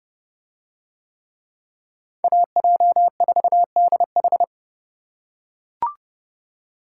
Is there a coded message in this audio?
Yes: Morse code from 2.2 to 4.4 s, heard as a keyed tone, about 29 wpm, 700 Hz. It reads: AJ4DH